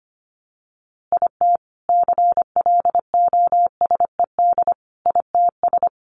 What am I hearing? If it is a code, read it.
Morse code, heard as a keyed tone, about 25 wpm, 700 Hz. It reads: IT CLOSED ITS
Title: A Sound Of Thunder 25wpm